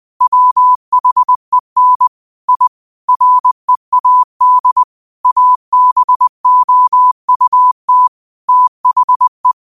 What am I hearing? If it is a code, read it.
Morse code, heard as a keyed tone, about 20 wpm, 1000 Hz. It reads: WHEN I READ ABOUT THE
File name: qotdfiles / 20wpm